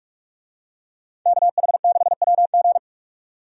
Morse code, heard as a keyed tone, about 46 wpm, 700 Hz. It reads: KH6WG
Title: DXCC-100 Call Signs - 5 Characters - 45wpm